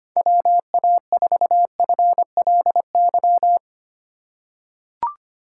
Morse code, heard as a keyed tone, about 25 wpm, 700 Hz. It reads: WA4FLY
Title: USA Call Signs - 2x Character Spacing - 25wpm